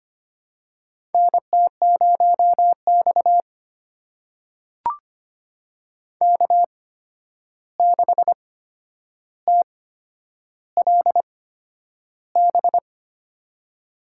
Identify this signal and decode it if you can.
Morse code, heard as a keyed tone, about 25 wpm, spaced as for 10 wpm, 700 Hz. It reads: NT0X K6TLB
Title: USA Call Signs - 8x Character Spacing - 25wpm